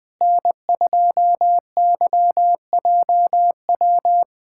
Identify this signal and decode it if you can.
Morse code, heard as a keyed tone, about 20 wpm, 700 Hz. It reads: N2YJW
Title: USA Callsign DE Callsign 20wpm